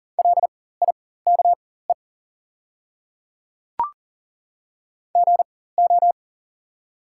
Morse code, heard as a keyed tone, about 40 wpm, spaced as for 19 wpm, 690 Hz. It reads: LIKE GO
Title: 4x Character Spacing - Top 100 Words - 40wpm